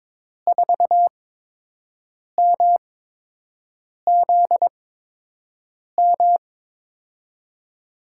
Morse code, heard as a keyed tone, about 22 wpm, spaced as for 6 wpm, 700 Hz. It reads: 4MZM
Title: USA Call Signs - 6 Characters - 8x Character Spacing - 22wpm